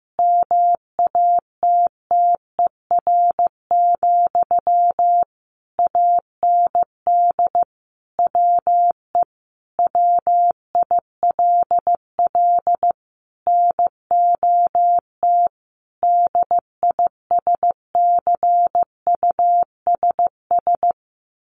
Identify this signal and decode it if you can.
Morse code, heard as a keyed tone, about 15 wpm, 700 Hz. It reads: MATTER, AND WE WILL NOT DISCUSS
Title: Sentences from Top 1100 Words 15wpm